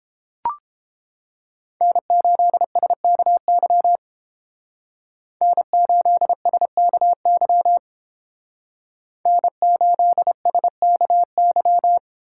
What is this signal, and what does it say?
Morse code, heard as a keyed tone, about 31 wpm, 700 Hz. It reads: N8SKY N8SKY N8SKY